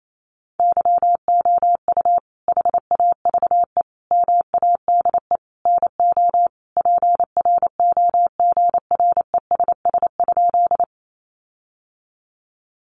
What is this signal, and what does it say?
Morse code, heard as a keyed tone, about 28 wpm, 700 Hz. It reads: YOU HAVE MADE NO PROGRESS?
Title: Sentences from Top 1400 Words 28wpm